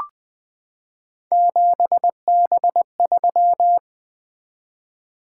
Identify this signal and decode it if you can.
Morse code, heard as a keyed tone, about 20 wpm, 700 Hz. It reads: 7B3
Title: Three Letter-Number Combinations - 20wpm